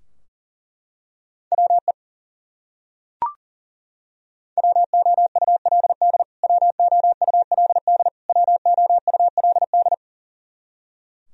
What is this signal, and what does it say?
Morse code, heard as a keyed tone, about 40 wpm, 700 Hz. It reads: WE WOULD WOULD WOULD
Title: Top 100 Words Repeated 40wpm